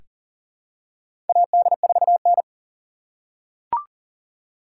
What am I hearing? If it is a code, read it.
Morse code, heard as a keyed tone, about 40 wpm, 700 Hz. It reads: AD4N
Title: USA Call Signs - 4x Character Spacing - 40wpm